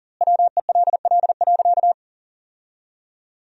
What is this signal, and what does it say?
Morse code, heard as a keyed tone, about 40 wpm, 700 Hz. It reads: WELL.
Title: 2x Word-Spacing - Sentences from Top 300 Words - 40wpm